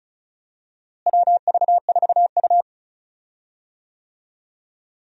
Morse code, heard as a keyed tone, about 34 wpm, 700 Hz. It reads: WV4U
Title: USA Call Signs - 4 Characters - Repeated 2x - 35wpm